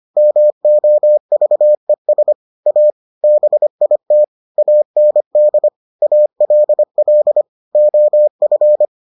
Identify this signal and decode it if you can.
Morse code, heard as a keyed tone, about 25 wpm, 600 Hz. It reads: MOVES A BIT AND ALL OF